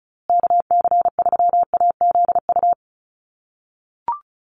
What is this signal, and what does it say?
Morse code, heard as a keyed tone, about 34 wpm, 700 Hz. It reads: KC3AZU
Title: USA Call Signs Repeated 2x 35wpm